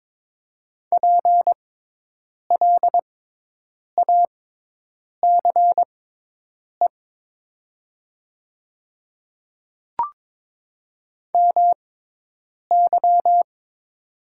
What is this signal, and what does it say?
Morse code, heard as a keyed tone, about 22 wpm, spaced as for 8 wpm, 700 Hz. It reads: PLACE MY
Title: 6x Character Spacing - Top 500 Words - 22wpm